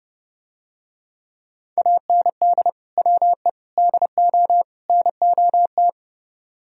Morse code, heard as a keyed tone, about 31 wpm, 700 Hz. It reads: AND WE DO NOT